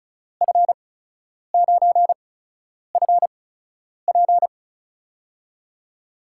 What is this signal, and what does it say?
Morse code, heard as a keyed tone, about 36 wpm, spaced as for 10 wpm, 700 Hz. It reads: F9FP
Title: USA Call Signs - 5 Characters - 8x Character Spacing - 35wpm